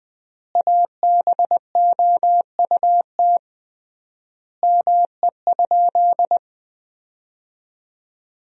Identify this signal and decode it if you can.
Morse code, heard as a keyed tone, about 20 wpm, 700 Hz. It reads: ABOUT ME?
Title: 3x Word-Spacing - Sentences from Top 200 Words - 20wpm